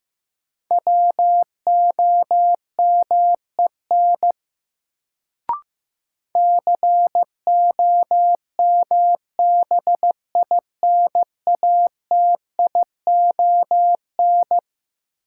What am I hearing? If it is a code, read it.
Morse code, heard as a keyed tone, about 15 wpm, 700 Hz. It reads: WOMEN COMBINATION
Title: Sentences from Top 1300 Words - New Word Drill Alt - 15wpm